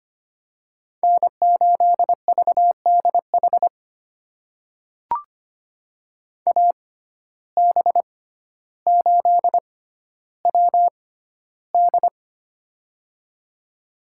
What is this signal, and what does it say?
Morse code, heard as a keyed tone, about 25 wpm, spaced as for 13 wpm, 700 Hz. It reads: N8VDH AB8WD
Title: USA Call Signs - 5 Characters - 6x Character Spacing - 25wpm